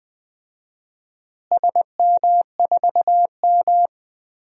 Morse code, heard as a keyed tone, about 19 wpm, 700 Hz. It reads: SM4M
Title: DXCC-100 Call Signs - 4 Characters - Repeated 3x - 20wpm